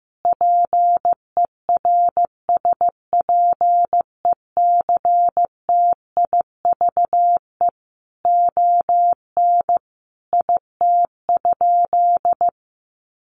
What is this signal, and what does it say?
Morse code, heard as a keyed tone, about 15 wpm, 700 Hz. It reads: PERSPECTIVE ON IT?
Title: Sentences from Top 1200 Words 15wpm